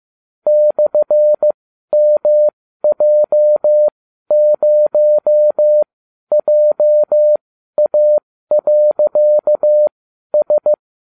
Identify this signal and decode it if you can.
Morse code, heard as a keyed tone, about 15 wpm, spaced as for 12 wpm, 600 Hz. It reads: /MJ0JA.S